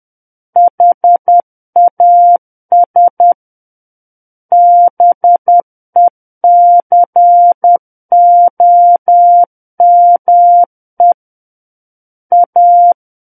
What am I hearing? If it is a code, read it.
Morse code, heard as a keyed tone, about 10 wpm, 700 Hz. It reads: HAS BECOME A
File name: Book One / 01-Chapter 1 - 10 WPM